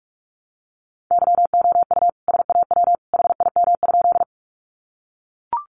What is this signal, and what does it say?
Morse code, heard as a keyed tone, about 44 wpm, 700 Hz. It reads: YOU SAW HIM?
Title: Sentences from Top 300 Words 45wpm